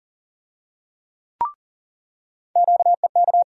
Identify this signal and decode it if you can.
Morse code, heard as a keyed tone, about 40 wpm, 700 Hz. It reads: QEK
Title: USA Call Sign Postfixes - 3 Letters 40wpm